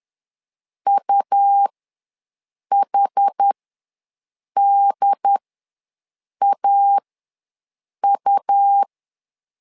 Morse code, heard as a keyed tone, about 10 wpm, spaced as for 6 wpm, 800 Hz. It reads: UHDAU
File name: LEZIONI / 2 UHDN / 2 UHD ws 10wpm 2058